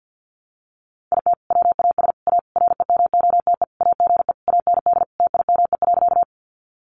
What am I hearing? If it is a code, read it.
Morse code, heard as a keyed tone, about 50 wpm, 700 Hz. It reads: IT WAS A REMOTE AGE AND TIME.
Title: Sentences from Top 1300 Words 50wpm